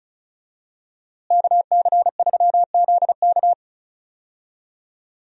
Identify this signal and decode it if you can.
Morse code, heard as a keyed tone, about 34 wpm, 700 Hz. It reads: KC3ZK
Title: USA Call Signs - 5 Characters - Repeated 2x - 35wpm